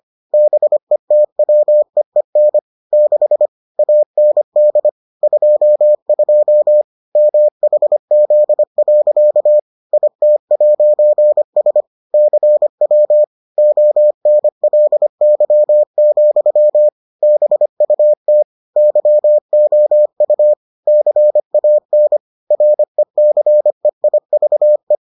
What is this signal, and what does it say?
Morse code, heard as a keyed tone, about 25 wpm, 600 Hz. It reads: BETWEEN 6 AND 22 MHZ. IT'S CW ONLY, BUT YOU CAN RECEIVE